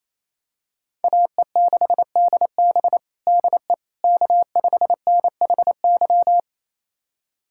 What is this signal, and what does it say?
Morse code, heard as a keyed tone, about 28 wpm, 700 Hz. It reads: AE6DB DE K5NHY
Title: USA Callsign DE Callsign - 5 Characters - 28wpm